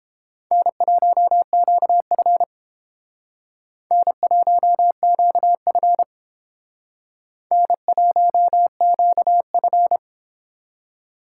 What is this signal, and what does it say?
Morse code, heard as a keyed tone, about 31 wpm, 700 Hz. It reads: N1QF N1QF N1QF